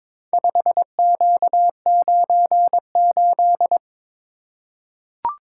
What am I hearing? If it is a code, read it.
Morse code, heard as a keyed tone, about 22 wpm, 700 Hz. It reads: 5Q98
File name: practice / 22wpm / Four Letter-Number Combinations - 22wpm